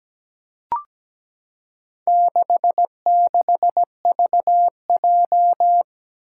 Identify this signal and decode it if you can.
Morse code, heard as a keyed tone, about 17 wpm, 700 Hz. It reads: 66VJ
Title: Four Challenging Letter-Number Combinations 17wpm